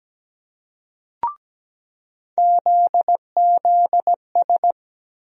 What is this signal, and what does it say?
Morse code, heard as a keyed tone, about 17 wpm, 700 Hz. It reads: ZZS
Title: USA Call Sign Postfixes - 3 Letters 17wpm